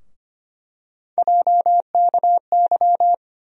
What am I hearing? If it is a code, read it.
Morse code, heard as a keyed tone, about 25 wpm, 700 Hz. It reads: JKY
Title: USA Call Sign Postfixes - 3 Letters 25wpm